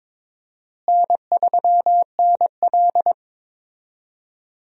Morse code, heard as a keyed tone, about 21 wpm, 700 Hz. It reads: N3NL
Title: USA Call Signs - 4 Characters - Repeated 2x - 22wpm